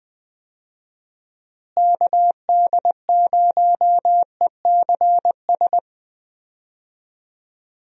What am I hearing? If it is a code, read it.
Morse code, heard as a keyed tone, about 20 wpm, 700 Hz. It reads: KD0ECS
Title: USA Call Signs - 6 Characters - Repeated 3x - 20wpm